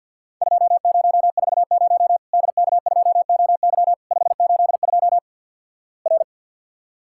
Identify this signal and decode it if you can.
Morse code, heard as a keyed tone, about 47 wpm, 700 Hz. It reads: 2040 DK2OY 582 R
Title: Worked All Europe DX Contest - QTC Exchange - 50wpm